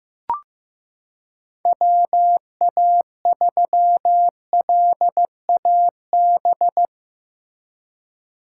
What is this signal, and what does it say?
Morse code, heard as a keyed tone, about 15 wpm, 700 Hz. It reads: WA3LAB
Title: K1USN SST QSO Practice - 15wpm